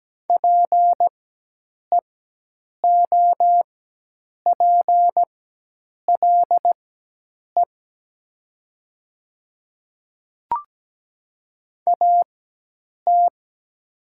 Morse code, heard as a keyed tone, about 17 wpm, spaced as for 8 wpm, 700 Hz. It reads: PEOPLE AT
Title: 4x Character Spacing - Top 100 Words - 17wpm